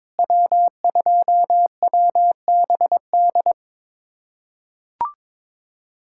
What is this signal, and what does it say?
Morse code, heard as a keyed tone, about 22 wpm, 700 Hz. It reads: W2WBD
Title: USA Callsign DE Callsign - 5 Characters - 22wpm